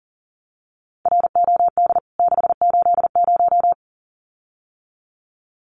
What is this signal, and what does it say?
Morse code, heard as a keyed tone, about 40 wpm, 700 Hz. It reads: ROD 680
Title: CWT Exchange 40wpm